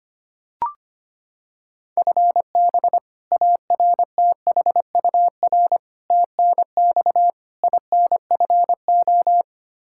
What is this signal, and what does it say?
Morse code, heard as a keyed tone, about 25 wpm, 700 Hz. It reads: FB ARTHUR TNX INFO
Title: Practice QSO - Exchange 3 - More Info - 25wpm